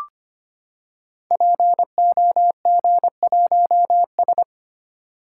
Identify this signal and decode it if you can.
Morse code, heard as a keyed tone, about 25 wpm, 700 Hz. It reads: POG1S
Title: Code Groups - 5 Letter:Numbers - 25wpm